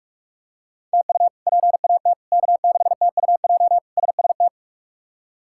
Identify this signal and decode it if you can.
Morse code, heard as a keyed tone, about 46 wpm, 700 Hz. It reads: TU PAT K6TUJ SST
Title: K1USN SST QSO Practice - 45wpm